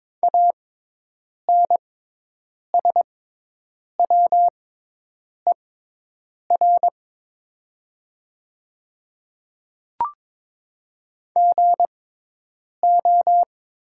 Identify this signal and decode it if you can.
Morse code, heard as a keyed tone, about 22 wpm, spaced as for 8 wpm, 700 Hz. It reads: ANSWER GO
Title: 6x Character Spacing - Top 300 Words - 22wpm